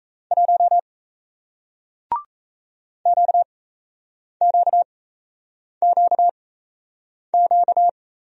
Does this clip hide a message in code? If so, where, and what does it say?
Yes: Morse code from 0.3 to 7.9 s, heard as a keyed tone, about 38 wpm, spaced as for 8 wpm, 700 Hz. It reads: 1 QQQQ